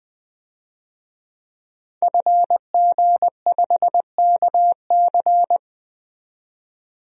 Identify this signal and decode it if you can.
Morse code, heard as a keyed tone, about 21 wpm, 700 Hz. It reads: FG5KC
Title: DXCC-100 Call Signs - 5 Characters - 20wpm